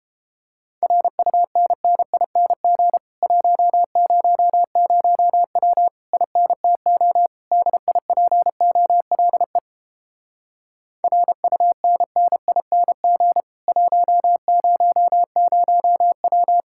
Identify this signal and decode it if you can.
Morse code, heard as a keyed tone, about 31 wpm, 700 Hz. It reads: RUNNING 100W INTO DIPOLE RUNNING 100W